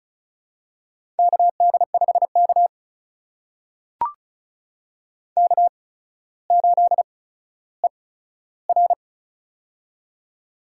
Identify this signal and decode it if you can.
Morse code, heard as a keyed tone, about 36 wpm, spaced as for 15 wpm, 700 Hz. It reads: KD5K K8ER
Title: USA Call Signs - 4 Characters - 8x Character Spacing - 35wpm